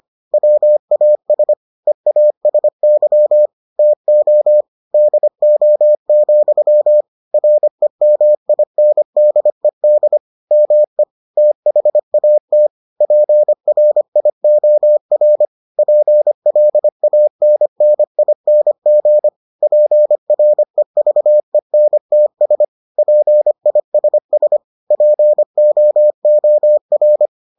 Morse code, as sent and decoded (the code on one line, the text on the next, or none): .-- .- ... / . .- ... -.-- / - --- / -.. --- --..-- / .-. -- .. -. -.. -.. / -- . / - .... .- - / .--. .-. .. --- .-. / .--. .-.. .- -. -. .. -. --. / .--. .-. ...- . -. - ... / .--. .. ... ... / .--. --- --- .-.
WAS EASY TO DO, REMINDED ME THAT PRIOR PLANNING PREVENTS PISS POOR